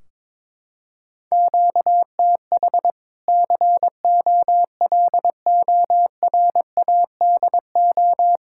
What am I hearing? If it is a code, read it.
Morse code, heard as a keyed tone, about 22 wpm, 700 Hz. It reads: QTH COLORADO
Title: Practice QSO - Exchange 1 - Beginning - 22wpm